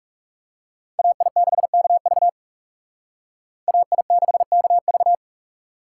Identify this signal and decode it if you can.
Morse code, heard as a keyed tone, about 38 wpm, 700 Hz. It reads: AI6KV AI6KV